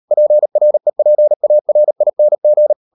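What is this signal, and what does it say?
Morse code, heard as a keyed tone, about 38 wpm, 600 Hz. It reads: PREPARING
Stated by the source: Morse code practice at 38 WPM, sending Ken Follett's novel Never in English, from CW med Gnister.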